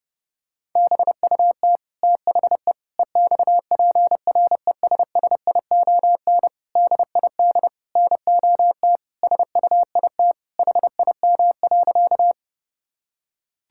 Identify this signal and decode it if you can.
Morse code, heard as a keyed tone, about 32 wpm, 700 Hz. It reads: BUT THE EXPRESSION DID NOT SUIT HIM.
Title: Sentences from Top 1600 Words 30wpm